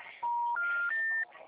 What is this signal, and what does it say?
Special information tone, with noise.